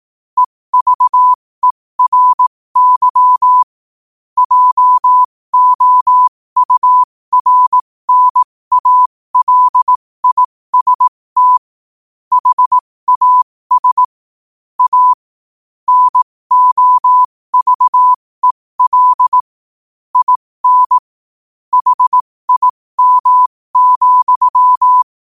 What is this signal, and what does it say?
Morse code, heard as a keyed tone, about 18 wpm, spaced as for 15 wpm, 1000 Hz. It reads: EVERY JOURNALIST HAS A NOVEL IN HIM,
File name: qotdfiles / 15wpm